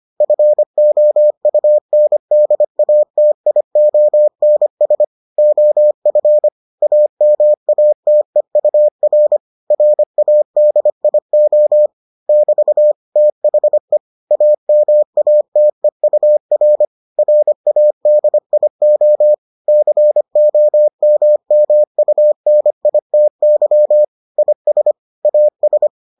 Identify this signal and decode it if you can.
Morse code, heard as a keyed tone, about 25 wpm, 600 Hz. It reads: FOUNDATIONS OF AMATEUR RADIO <BT> THE AMATEUR RADIO COMMUNITY IS AS